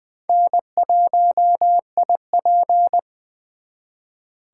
Morse code, heard as a keyed tone, about 23 wpm, 700 Hz. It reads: N1IP
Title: Speed Racing - USA Call Signs 20wpm